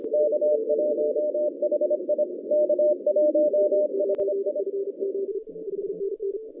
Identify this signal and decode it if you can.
Morse code, heard as a keyed tone, about 25 wpm, 600 Hz. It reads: K1HI K1HI
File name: 2015 cq ww cw / 2015 cqwww k5tr - 0-27